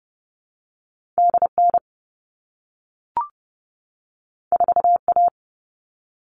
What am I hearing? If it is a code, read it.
Morse code, heard as a keyed tone, about 30 wpm, 700 Hz. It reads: D N 4 A
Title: All Cut Numbers - 2-Digit 30wpm